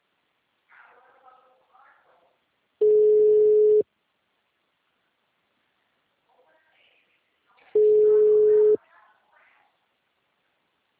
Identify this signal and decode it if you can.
Ringback tone (Europe, 425 Hz), with speech.
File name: out- -2004- - - .57000